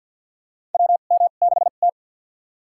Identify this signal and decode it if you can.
Morse code, heard as a keyed tone, about 48 wpm, spaced as for 35 wpm, 700 Hz. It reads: WM6T
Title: USA Call Signs - 4 Characters - 2x Character Spacing - 50wpm